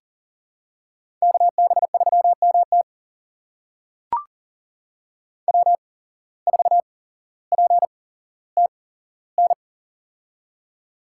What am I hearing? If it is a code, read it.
Morse code, heard as a keyed tone, about 40 wpm, spaced as for 17 wpm, 700 Hz. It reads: KB3MT W4PTN
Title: USA Call Signs - 5 Characters - 8x Character Spacing - 40wpm